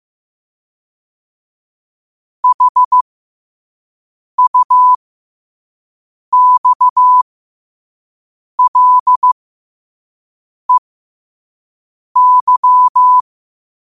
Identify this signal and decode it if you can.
Morse code, heard as a keyed tone, about 15 wpm, spaced as for 5 wpm, 1000 Hz. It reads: HUXLEY